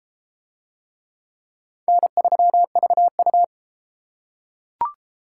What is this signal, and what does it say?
Morse code, heard as a keyed tone, about 28 wpm, 700 Hz. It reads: N3VU